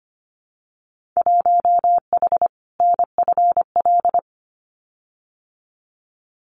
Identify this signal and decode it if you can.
Morse code, heard as a keyed tone, about 25 wpm, 700 Hz. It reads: 1H NFL
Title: Winter Field Day Exchanges 25wpm